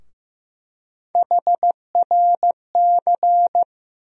Morse code, heard as a keyed tone, about 15 wpm, 700 Hz. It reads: HRC
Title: USA Call Sign Postfixes - 3 Letters 15wpm